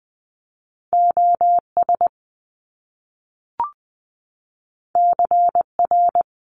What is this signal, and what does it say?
Morse code, heard as a keyed tone, about 20 wpm, 700 Hz. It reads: OS CR
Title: Top Two Letter Combinations in Words 20wpm